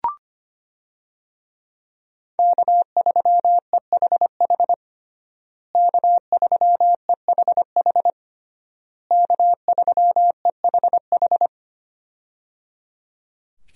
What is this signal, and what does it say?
Morse code, heard as a keyed tone, about 25 wpm, 700 Hz. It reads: K3EHH K3EHH K3EHH
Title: USA Call Signs - 5 Characters - Repeated 3x - 25wpm